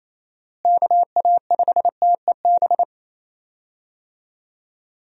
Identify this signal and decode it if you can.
Morse code, heard as a keyed tone, about 28 wpm, 700 Hz. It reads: KA5TEB
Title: USA Call Signs - 6 Characters - Repeated 2x - 28wpm